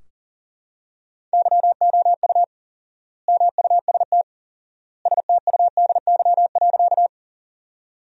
Morse code, heard as a keyed tone, about 40 wpm, 700 Hz. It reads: YOU MUST STUDY.
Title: 4x Word-Spacing - Sentences from Top 300 Words - 40wpm